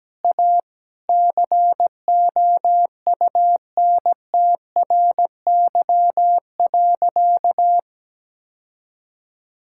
Morse code, heard as a keyed tone, about 17 wpm, 700 Hz. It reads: A COUNTRY.
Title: Sentences from Top 1600 Words 17wpm